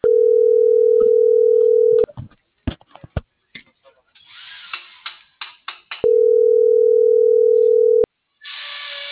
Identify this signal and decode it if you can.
Ringback tone (North America), 440 + 480 Hz.